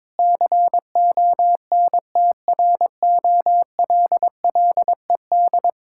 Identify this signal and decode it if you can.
Morse code, heard as a keyed tone, about 21 wpm, 690 Hz. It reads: CONTROLLED
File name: Difficult Words with Hints - 22wpm